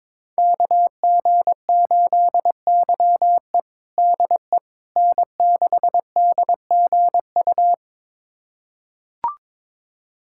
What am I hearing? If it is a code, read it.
Morse code, heard as a keyed tone, about 22 wpm, 700 Hz. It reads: KG8YE DE N6DGU
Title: USA Callsign DE Callsign - 5 Characters - 22wpm